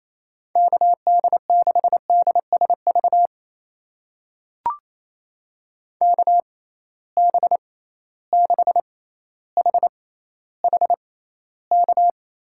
Morse code, heard as a keyed tone, about 28 wpm, spaced as for 14 wpm, 700 Hz. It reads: KD6DSV KB6HHK
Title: USA Call Signs - 6 Characters - 6x Character Spacing - 28wpm